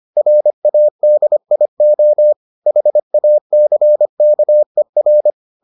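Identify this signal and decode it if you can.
Morse code, heard as a keyed tone, about 25 wpm, 600 Hz. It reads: RADIO HACKER
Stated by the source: Morse edition of the Foundations of Amateur Radio episodes.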